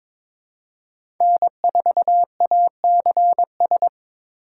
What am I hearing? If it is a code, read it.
Morse code, heard as a keyed tone, about 22 wpm, 700 Hz. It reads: N4ACS
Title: USA Call Signs - 5 Characters - 3x Character Spacing - 22wpm